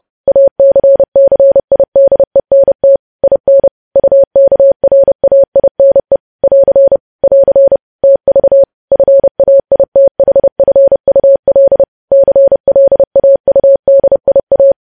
Morse code, heard as a keyed tone, about 30 wpm, 550 Hz. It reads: ACCIDENT IN UKRAINE <AR> <AR> TV FAITHFUL CLAUDIA